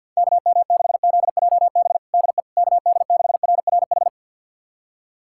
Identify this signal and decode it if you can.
Morse code, heard as a keyed tone, about 47 wpm, 700 Hz. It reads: KM6ZJB DE KD6RRH
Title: USA Callsign DE Callsign 50wpm